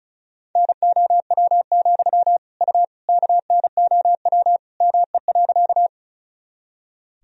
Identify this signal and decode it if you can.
Morse code, heard as a keyed tone, about 34 wpm, 700 Hz. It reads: NOW, U KNOW ME.
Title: Sentences from Top 200 Words and Abbreviations 35wpm